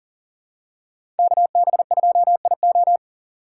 Morse code, heard as a keyed tone, about 40 wpm, 700 Hz. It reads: KB2IO
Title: USA Call Signs - 5 Characters - 3x Character Spacing - 40wpm